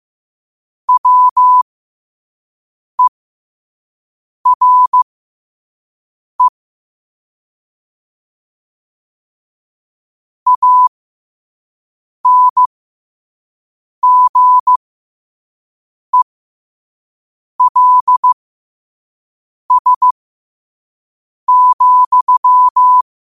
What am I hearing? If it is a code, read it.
Morse code, heard as a keyed tone, about 15 wpm, spaced as for 5 wpm, 1000 Hz. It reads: WERE ANGELS,
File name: qotdfiles / 5wpm